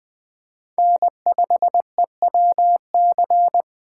Morse code, heard as a keyed tone, about 20 wpm, 700 Hz. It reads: N5EWC
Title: Speed Racing - USA 1x3 Call Signs 17wpm